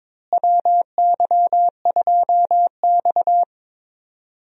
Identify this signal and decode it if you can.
Morse code, heard as a keyed tone, about 22 wpm, 700 Hz. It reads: WY2X
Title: USA Call Signs - 4 Characters - Repeated 2x - 22wpm